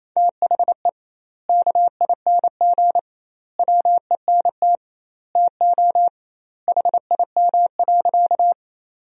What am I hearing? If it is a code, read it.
Morse code, heard as a keyed tone, about 28 wpm, 700 Hz. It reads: THE KING WENT TO HIM.
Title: 2x Word-Spacing - Sentences from Top 400 Words - 28wpm